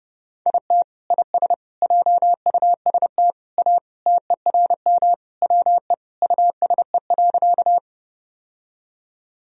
Morse code, heard as a keyed tone, about 30 wpm, 700 Hz. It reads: IT IS JUST A TERM WE USE.
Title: Sentences from Top 1000 Words - Review Part 4 - 30wpm